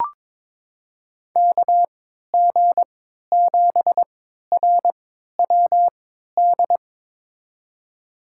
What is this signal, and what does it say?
Morse code, heard as a keyed tone, about 22 wpm, spaced as for 12 wpm, 700 Hz. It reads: KG7RWD
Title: USA Call Signs - 6 Characters - 3x Character Spacing - 22wpm